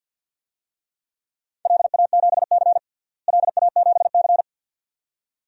Morse code, heard as a keyed tone, about 53 wpm, 700 Hz. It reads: LA7C LA7C
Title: DXCC-100 Call Signs - 4 Characters - Repeated 2x - 50wpm